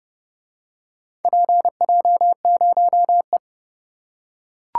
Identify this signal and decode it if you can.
Morse code, heard as a keyed tone, about 30 wpm, 700 Hz. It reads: PJ0E